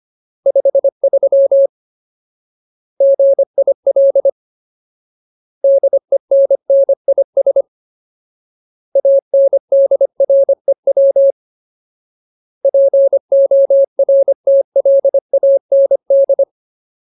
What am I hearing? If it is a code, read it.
Morse code, heard as a keyed tone, about 25 wpm, 550 Hz. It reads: 53 GIL DENNIS ANDREW PORTLAND
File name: Ultimate Word List / 25WPM Ultimate